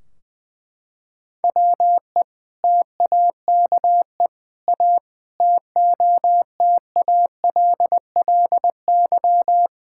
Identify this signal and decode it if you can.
Morse code, heard as a keyed tone, about 21 wpm, 700 Hz. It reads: WE TAKE A TOTALLY
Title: Sentences from Top 1600 Words 20wpm